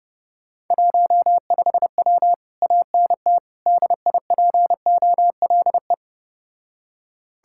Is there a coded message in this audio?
Yes: Morse code from 0.7 to 5.9 s, heard as a keyed tone, about 32 wpm, 700 Hz. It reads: 15W ANT DIPOLE